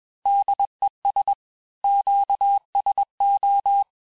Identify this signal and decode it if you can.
Morse code, heard as a keyed tone, about 21 wpm, 800 Hz. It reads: DES QSO
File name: teste-FIN-de-QSO-vitesse-20mpm